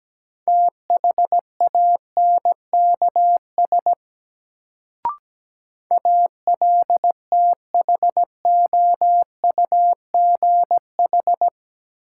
Morse code, heard as a keyed tone, about 17 wpm, 700 Hz. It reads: THANKS ALTHOUGH
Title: Sentences from Top 600 Words - New Word Drill - 17wpm